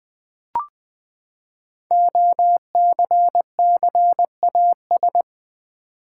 Morse code, heard as a keyed tone, about 20 wpm, 700 Hz. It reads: OCCAS
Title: Build A Word - 5 Letters At A Time - 12-Letter Words - 20wpm